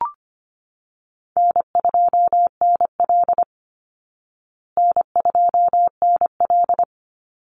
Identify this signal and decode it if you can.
Morse code, heard as a keyed tone, about 25 wpm, 700 Hz. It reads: N2NL N2NL
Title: Call Signs Repeated 3x - 25wpm